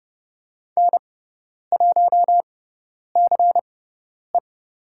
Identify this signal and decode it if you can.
Morse code, heard as a keyed tone, about 30 wpm, spaced as for 10 wpm, 700 Hz. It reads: N1CE
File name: Lesson 16g - Letter c - Call Signs 30wpm